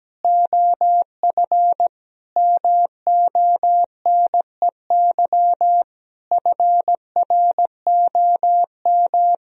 Morse code, heard as a keyed tone, about 17 wpm, 700 Hz. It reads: OF MONEY FROM
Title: Sentences from Top 1000 Words - Review Part 4 - 17wpm